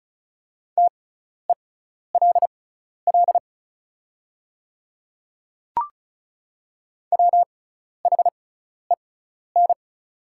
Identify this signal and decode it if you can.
Morse code, heard as a keyed tone, about 36 wpm, spaced as for 12 wpm, 690 Hz. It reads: TELL WHEN
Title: 6x Character Spacing - Top 300 Words - 35wpm